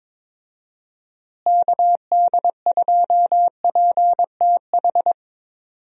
Morse code, heard as a keyed tone, about 21 wpm, 700 Hz. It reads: KD2PTH
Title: USA Call Signs - 6 Characters - Repeated 3x - 22wpm